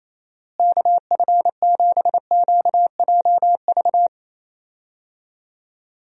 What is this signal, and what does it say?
Morse code, heard as a keyed tone, about 28 wpm, 700 Hz. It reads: KF7QJV
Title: USA Call Signs - 6 Characters - Repeated 3x - 28wpm